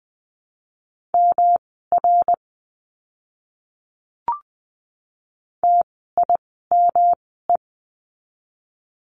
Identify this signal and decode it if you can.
Morse code, heard as a keyed tone, about 20 wpm, spaced as for 15 wpm, 700 Hz. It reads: MR TIME
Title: 244 Lesson 39f - Letter z - Words 20wpm